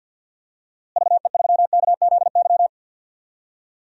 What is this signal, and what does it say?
Morse code, heard as a keyed tone, about 52 wpm, 700 Hz. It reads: VE3KZY
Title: DXCC-100 Call Signs - 6 Characters - Repeated 2x - 50wpm